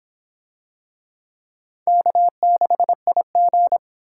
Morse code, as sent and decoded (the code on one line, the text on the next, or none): -.- -.... .. --.
K6IG